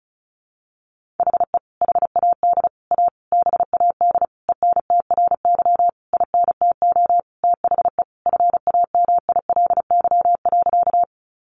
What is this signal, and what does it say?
Morse code, heard as a keyed tone, about 34 wpm, 700 Hz. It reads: HE HAD A BAD ENTRY INTO THE FAMILY.
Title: Sentences from Top 1500 Words 35wpm